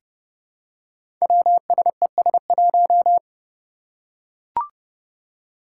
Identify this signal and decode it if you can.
Morse code, heard as a keyed tone, about 30 wpm, 700 Hz. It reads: WSES1
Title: Five Challenging Letter-Number Combinations 30wpm